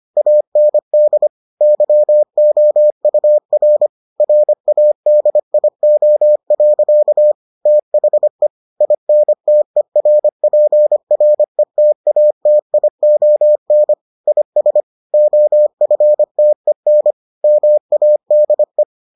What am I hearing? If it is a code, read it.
Morse code, heard as a keyed tone, about 25 wpm, 600 Hz. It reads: AND YOUR RADIO. THE INTERPRETATION IS OFTEN MADE